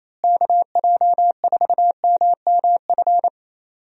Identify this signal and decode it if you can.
Morse code, heard as a keyed tone, about 27 wpm, 700 Hz. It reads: KJ4MMF
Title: USA Call Signs - 6 Characters - Repeated 3x - 28wpm